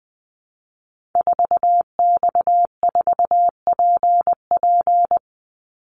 Morse code, heard as a keyed tone, about 20 wpm, 700 Hz. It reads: 4X4PP
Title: Call Signs - 20wpm